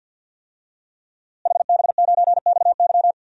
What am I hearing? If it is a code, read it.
Morse code, heard as a keyed tone, about 53 wpm, 700 Hz. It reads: HB9XY
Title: DXCC-100 Call Signs - 5 Characters - Repeated 3x - 50wpm